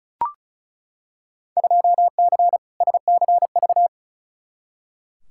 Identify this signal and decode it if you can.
Morse code, heard as a keyed tone, about 36 wpm, 690 Hz. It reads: 2C SCV
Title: Field Day Exchanges Encore - 35wpm